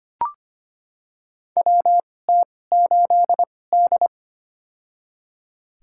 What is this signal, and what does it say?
Morse code, heard as a keyed tone, about 25 wpm, spaced as for 18 wpm, 700 Hz. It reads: WT8D
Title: USA Call Signs - 4 Characters - 2x Character Spacing - 25wpm